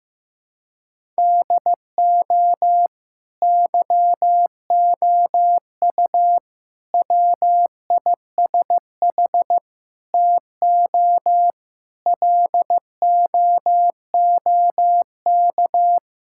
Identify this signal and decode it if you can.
Morse code, heard as a keyed tone, about 16 wpm, 700 Hz. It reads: DO YOU WISH TO LOOK
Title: Sentences from Top 1600 Words 15wpm